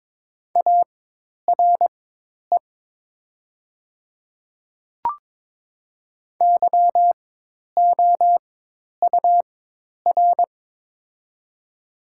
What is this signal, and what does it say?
Morse code, heard as a keyed tone, about 22 wpm, spaced as for 10 wpm, 700 Hz. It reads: ARE YOUR
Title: 4x Character Spacing - Top 100 Words - 22wpm